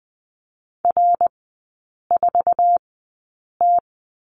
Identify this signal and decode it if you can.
Morse code, heard as a keyed tone, about 20 wpm, spaced as for 8 wpm, 690 Hz. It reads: R4T
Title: 081 Lesson 14e - Number 2 - Three Characters 20wpm